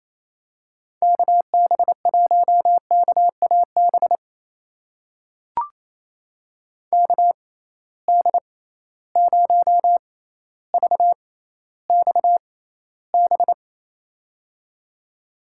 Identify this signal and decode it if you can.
Morse code, heard as a keyed tone, about 28 wpm, spaced as for 14 wpm, 700 Hz. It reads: KB1KAB KD0VXB
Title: USA Call Signs - 6 Characters - 6x Character Spacing - 28wpm